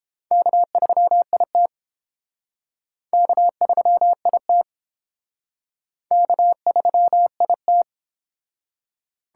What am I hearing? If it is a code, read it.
Morse code, heard as a keyed tone, about 29 wpm, 700 Hz. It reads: K3IT K3IT K3IT